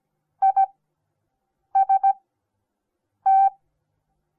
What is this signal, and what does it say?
Morse code, heard as a keyed tone, about 17 wpm, spaced as for 6 wpm, 780 Hz. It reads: IST
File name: Onlinelehrgang / morsen / DJ4UF Morsen08